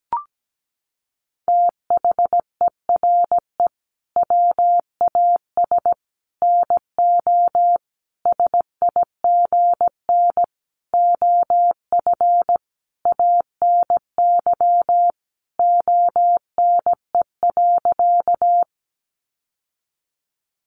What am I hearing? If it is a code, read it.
Morse code, heard as a keyed tone, about 17 wpm, 700 Hz. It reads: THERE WAS NO SIGN OF ANY ONE.
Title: Sentences from Top 1000 Words 17wpm